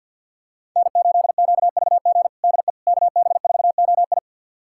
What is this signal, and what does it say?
Morse code, heard as a keyed tone, about 51 wpm, 700 Hz. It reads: N8QVG DE KB4OI
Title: USA Callsign DE Callsign - 5 Characters - 50wpm